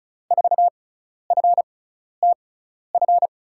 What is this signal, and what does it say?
Morse code, heard as a keyed tone, about 36 wpm, spaced as for 12 wpm, 700 Hz. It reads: 4FTF
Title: USA Call Signs - 6 Characters - 6x Character Spacing - 35wpm